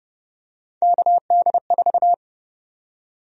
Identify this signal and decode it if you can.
Morse code, heard as a keyed tone, about 30 wpm, 700 Hz. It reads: KD4
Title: USA Call Sign Prefixes - 30wpm